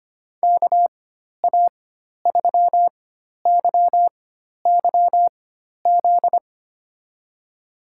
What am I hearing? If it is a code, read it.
Morse code, heard as a keyed tone, about 25 wpm, spaced as for 12 wpm, 700 Hz. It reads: KA3YYZ
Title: USA Call Signs - 6 Characters - 4x Character Spacing - 25wpm